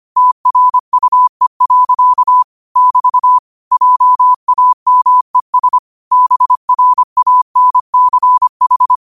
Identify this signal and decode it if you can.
Morse code, heard as a keyed tone, about 25 wpm, 1000 Hz. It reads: TRUE. <BT> JAMES BRANCH
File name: qotdfiles / 25wpm